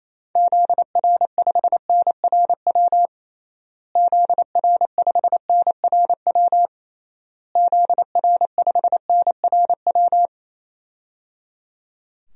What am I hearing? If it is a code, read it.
Morse code, heard as a keyed tone, about 28 wpm, 700 Hz. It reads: ZR5NRW ZR5NRW ZR5NRW
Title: DXCC-100 Call Signs - 6 Characters - Repeated 3x - 28wpm